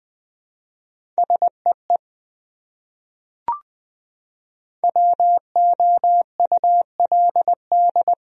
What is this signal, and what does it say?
Morse code, heard as a keyed tone, about 21 wpm, 700 Hz. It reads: SEE WOULD
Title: Top 100 Words Repeated 20wpm